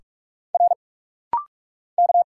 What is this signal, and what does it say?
Morse code, heard as a keyed tone, about 45 wpm, 700 Hz. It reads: R K
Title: Single Letter-Number Alt - Rapid-Fire 45wpm